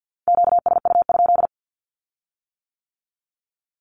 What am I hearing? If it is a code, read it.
Morse code, heard as a keyed tone, about 50 wpm, 700 Hz. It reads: QSU?
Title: Q-Codes with Spoken Meaning - 50wpm